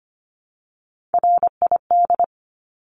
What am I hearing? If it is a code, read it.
Morse code, heard as a keyed tone, about 25 wpm, 700 Hz. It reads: RID
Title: Top 1751-2000 Words 25wpm